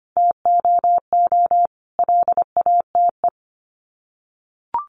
Morse code, heard as a keyed tone, about 25 wpm, 700 Hz. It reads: TOO LATE
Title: Sets of 5 Words Encore2 25wpm